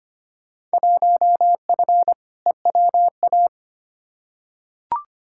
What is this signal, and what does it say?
Morse code, heard as a keyed tone, about 25 wpm, 700 Hz. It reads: 1F EWA
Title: Field Day Exchanges Encore - 25wpm